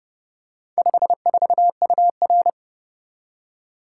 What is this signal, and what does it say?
Morse code, heard as a keyed tone, about 30 wpm, 700 Hz. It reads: 54UR